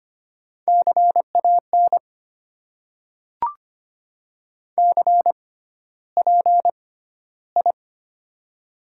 Morse code, heard as a keyed tone, about 25 wpm, spaced as for 13 wpm, 700 Hz. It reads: CAN CPI
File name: Top 90 QSO Elements - 6x Character Spacing - 25wpm